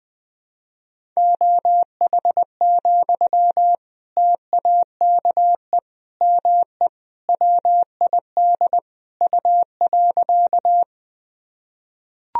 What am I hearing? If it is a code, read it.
Morse code, heard as a keyed tone, about 19 wpm, 700 Hz. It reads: OH, TAKE ME WID U.
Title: Sentences from Top 500 Words and Abbreviations 20wpm